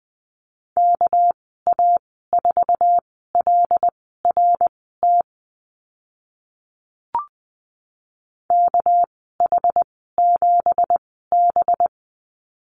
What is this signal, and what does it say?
Morse code, heard as a keyed tone, about 20 wpm, spaced as for 14 wpm, 700 Hz. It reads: KA4LRT KH7B
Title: 207 Lesson 33g - Letter j - Call Signs 20wpm